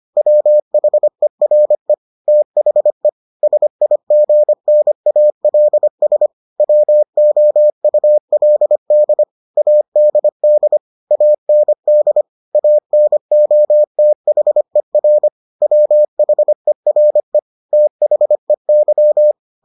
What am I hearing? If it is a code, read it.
Morse code, heard as a keyed tone, about 25 wpm, 600 Hz. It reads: WHERE THE SIGNALS WOULD ADD AND ANOTHER WHERE THEY